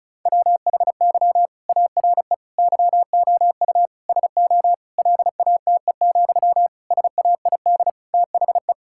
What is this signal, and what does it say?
Morse code, heard as a keyed tone, about 36 wpm, 700 Hz. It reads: WHY ARE YOU SO LATE, SAID THE
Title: Sentences from Top 1000 Words - Review Part 6 - 35wpm